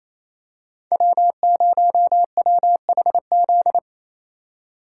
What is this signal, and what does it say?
Morse code, heard as a keyed tone, about 28 wpm, 700 Hz. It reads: W0WHZ
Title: USA Call Signs - 5 Characters - 4x Character Spacing - 28wpm